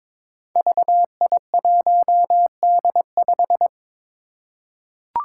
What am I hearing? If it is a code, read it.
Morse code, heard as a keyed tone, about 22 wpm, 700 Hz. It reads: VI1D5
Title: Five Challenging Letter-Number Combinations 22wpm